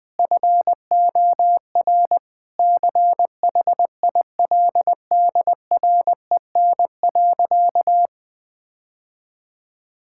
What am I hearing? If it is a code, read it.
Morse code, heard as a keyed tone, about 20 wpm, 700 Hz. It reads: FOR CHILDREN.
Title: Sentences from Top 1000 Words - Review Part 5 - 20wpm